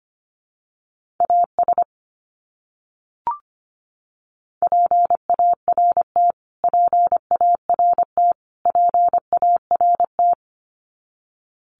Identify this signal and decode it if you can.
Morse code, heard as a keyed tone, about 25 wpm, 700 Hz. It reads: AS PART PART PART
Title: Top 200 Words Repeated 25wpm